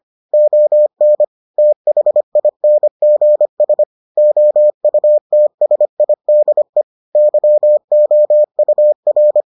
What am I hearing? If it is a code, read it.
Morse code, heard as a keyed tone, about 25 wpm, 600 Hz. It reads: ON THINGS OUTSIDE YOUR